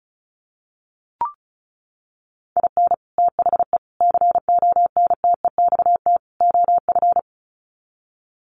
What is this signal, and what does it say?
Morse code, heard as a keyed tone, about 34 wpm, 700 Hz. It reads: IN THE CONTEXT OF
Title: Sets of 4 Words 35wpm